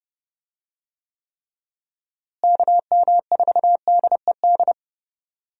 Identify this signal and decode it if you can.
Morse code, heard as a keyed tone, about 30 wpm, 700 Hz. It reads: KM4DED
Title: USA Call Signs - 6 Characters - Repeated 3x - 30wpm